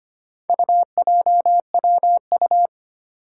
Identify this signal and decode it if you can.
Morse code, heard as a keyed tone, about 25 wpm, 700 Hz. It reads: UJWU
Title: Four Challenging Letter-Number Combinations 25wpm